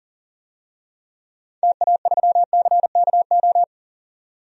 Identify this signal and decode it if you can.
Morse code, heard as a keyed tone, about 40 wpm, 700 Hz. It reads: TA3CKO
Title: DXCC-100 Call Signs - 6 Characters - 40wpm